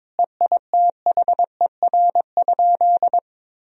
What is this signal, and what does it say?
Morse code, heard as a keyed tone, about 22 wpm, 700 Hz. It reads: EITHER?
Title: Sentences from Top 1000 Words - Review Part 4 - 22wpm